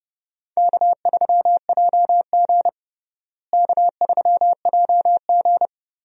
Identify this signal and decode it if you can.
Morse code, heard as a keyed tone, about 30 wpm, 700 Hz. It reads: K3JG K3JG
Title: USA Call Signs - 4 Characters - Repeated 3x - 30wpm